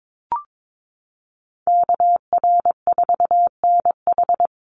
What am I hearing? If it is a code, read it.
Morse code, heard as a keyed tone, about 22 wpm, 700 Hz. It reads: KR4NH
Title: USA Call Signs Repeated 3x 22wpm